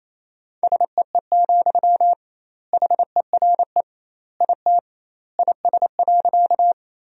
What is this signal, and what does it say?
Morse code, heard as a keyed tone, about 28 wpm, 700 Hz. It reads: SEE, HERE IT IS.
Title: 2x Word-Spacing - Sentences from Top 200 Words - 28wpm